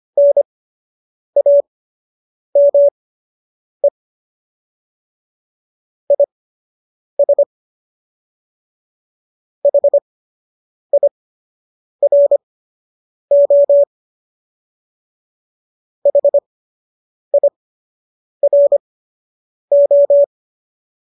Morse code, heard as a keyed tone, about 25 wpm, spaced as for 8 wpm, 570 Hz. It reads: NAME IS HIRO HIRO